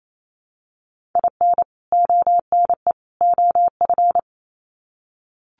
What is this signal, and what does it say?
Morse code, heard as a keyed tone, about 28 wpm, 700 Hz. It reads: IN ONE OF
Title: Sets of 3 Words 28wpm